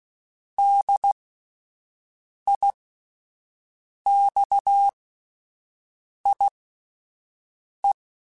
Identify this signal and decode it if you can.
Morse code, heard as a keyed tone, about 16 wpm, spaced as for 5 wpm, 800 Hz. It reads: DIXIE